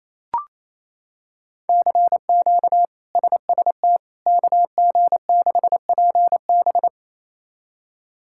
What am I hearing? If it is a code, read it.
Morse code, heard as a keyed tone, about 28 wpm, 700 Hz. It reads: CQ SST KG6PB
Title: K1USN SST QSO Practice - 28wpm